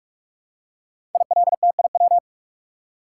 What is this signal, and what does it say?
Morse code, heard as a keyed tone, about 44 wpm, 700 Hz. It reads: ILTIW
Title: Code Groups - 5 Letters - 45wpm